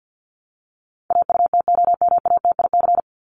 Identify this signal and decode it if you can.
Morse code, heard as a keyed tone, about 50 wpm, 700 Hz. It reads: AUTOMATIC
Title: Top 1751-2000 Words 50wpm